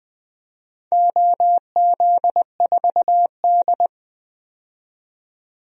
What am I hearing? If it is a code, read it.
Morse code, heard as a keyed tone, about 21 wpm, 700 Hz. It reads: OZ4D
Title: DXCC-100 Call Signs - 4 Characters - 20wpm